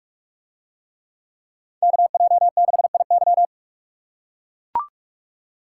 Morse code, heard as a keyed tone, about 44 wpm, 700 Hz. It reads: KJ6IY